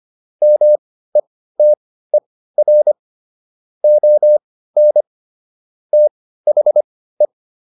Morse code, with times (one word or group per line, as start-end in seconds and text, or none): 0.4-2.9 METER
3.8-5.0 ON
5.9-7.3 THE